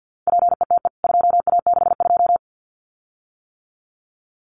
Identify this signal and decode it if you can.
Morse code, heard as a keyed tone, about 48 wpm, 700 Hz. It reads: PETE 2A62
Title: CWT Exchange 50wpm